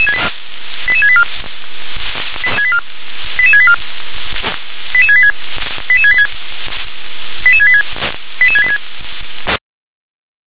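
Five-tone selective call (ZVEI), heard as quick runs of five tones, tones of about 70 ms. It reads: 88684 88684 88686 88686 88686 88686
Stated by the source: ZVEI five-tone call received by SDR.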